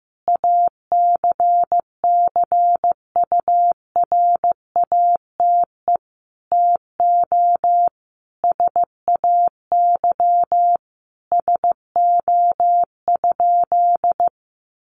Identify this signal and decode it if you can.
Morse code, heard as a keyed tone, about 15 wpm, 700 Hz. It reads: ACCURATE TO SAY SO?
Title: Sentences from Top 1200 Words 15wpm